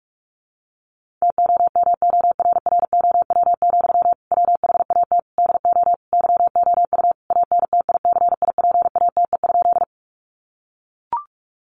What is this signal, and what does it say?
Morse code, heard as a keyed tone, about 44 wpm, 700 Hz. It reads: TOMORROW, WHAT DO YOU ANTICIPATE?
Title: Sentences from Top 1500 Words 45wpm